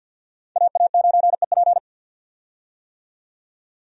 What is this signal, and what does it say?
Morse code, heard as a keyed tone, about 47 wpm, 700 Hz. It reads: AA9EP
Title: USA Call Signs - 5 Characters - 50wpm